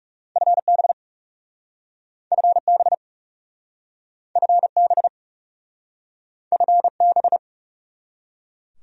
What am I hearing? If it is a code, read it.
Morse code, heard as a keyed tone, about 41 wpm, 700 Hz. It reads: FB FB FB FB